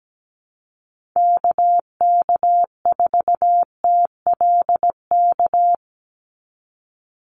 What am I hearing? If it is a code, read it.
Morse code, heard as a keyed tone, about 17 wpm, 700 Hz. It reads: KK4TLK
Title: USA Call Signs 17wpm